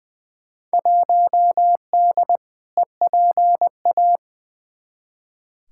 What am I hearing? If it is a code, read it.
Morse code, heard as a keyed tone, about 20 wpm, 700 Hz. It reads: 1D EPA
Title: Field Day Exchanges - 20wpm